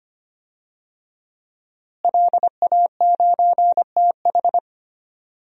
Morse code, heard as a keyed tone, about 25 wpm, 700 Hz. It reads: LA9TH
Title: DXCC-100 Call Signs - 5 Characters - Repeated 3x - 25wpm